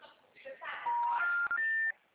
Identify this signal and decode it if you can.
Special information tone, with speech.
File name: out- -2017- - - .81255